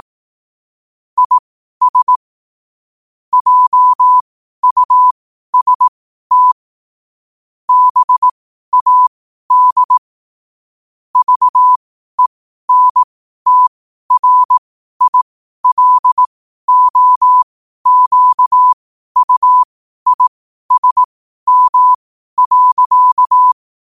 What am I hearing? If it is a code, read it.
Morse code, heard as a keyed tone, about 18 wpm, spaced as for 12 wpm, 1000 Hz. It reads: IS JUST BAD VENTRILOQUISM.
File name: qotdfiles / 12wpm